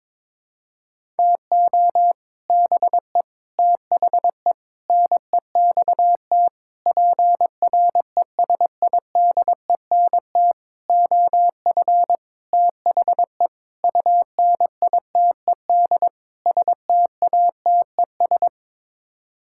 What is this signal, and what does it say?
Morse code, heard as a keyed tone, about 21 wpm, 700 Hz. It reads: TO BE THE NEXT PRESIDENT OF THE UNITED STATES
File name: Sets of 9 Words - 22wpm